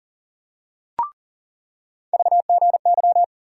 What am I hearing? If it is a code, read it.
Morse code, heard as a keyed tone, about 40 wpm, 700 Hz. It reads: VGY
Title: USA Call Sign Postfixes - 3 Letters 40wpm